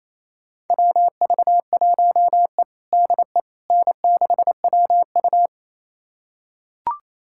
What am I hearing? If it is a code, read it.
Morse code, heard as a keyed tone, about 27 wpm, 700 Hz. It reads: WV1E DE N6WU
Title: USA Callsign DE Callsign - 4 Characters - 28wpm